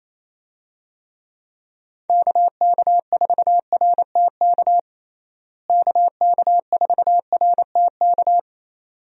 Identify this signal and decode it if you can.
Morse code, heard as a keyed tone, about 29 wpm, 700 Hz. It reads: KK4RTK KK4RTK
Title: USA Call Signs - 6 Characters - Repeated 2x - 28wpm